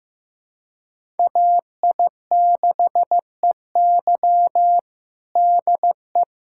Morse code, heard as a keyed tone, about 15 wpm, 700 Hz. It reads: AI6EY DE
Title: USA Callsign DE Callsign - 5 Characters - 15wpm